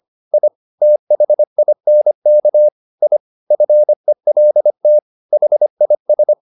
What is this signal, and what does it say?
Morse code, heard as a keyed tone, about 25 wpm, 600 Hz. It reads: I THINK I FELT HIS